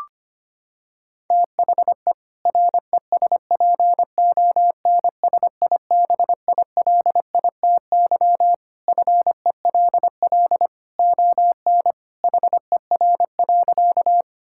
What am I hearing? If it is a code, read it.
Morse code, heard as a keyed tone, about 25 wpm, 700 Hz. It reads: THE RESPONSIBILITY FELL ON HER.
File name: practice / 25wpm / Sentences with 2-Word Phrases - 25wpm